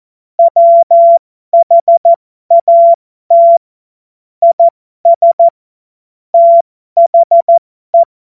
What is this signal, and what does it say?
Morse code, heard as a keyed tone, about 14 wpm, spaced as for 12 wpm, 680 Hz. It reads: WHAT IS THE
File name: Morsekurs / Witze